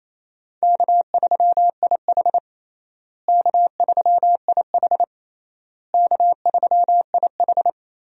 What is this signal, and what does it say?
Morse code, heard as a keyed tone, about 28 wpm, 700 Hz. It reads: K3IH K3IH K3IH
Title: USA Call Signs - 4 Characters - Repeated 3x - 28wpm